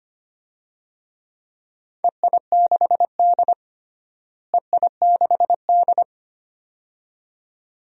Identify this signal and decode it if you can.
Morse code, heard as a keyed tone, about 25 wpm, 700 Hz. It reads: EI6D EI6D
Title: DXCC-100 Call Signs - 4 Characters - Repeated 2x - 25wpm